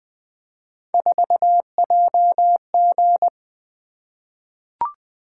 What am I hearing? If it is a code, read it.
Morse code, heard as a keyed tone, about 20 wpm, 700 Hz. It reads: 4JG
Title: Three Letter-Number Combinations - 20wpm